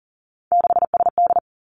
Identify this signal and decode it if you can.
Morse code, heard as a keyed tone, about 41 wpm, 690 Hz. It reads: 6SD
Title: Three Challenging Letter-Number Combinations 40wpm